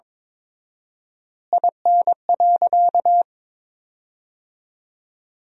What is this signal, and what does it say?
Morse code, heard as a keyed tone, about 22 wpm, 700 Hz. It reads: IN.
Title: 4x Word-Spacing - Sentences from Top 300 Words - 22wpm